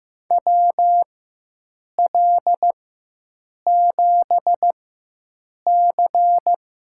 Morse code, heard as a keyed tone, about 15 wpm, spaced as for 7 wpm, 700 Hz. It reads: WL7C
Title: USA Call Signs - 5 Characters - 4x Character Spacing - 15wpm